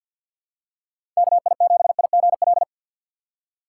Morse code, heard as a keyed tone, about 51 wpm, 700 Hz. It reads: KI7IGL
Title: USA Call Signs - 6 Characters - 6x Character Spacing - 50wpm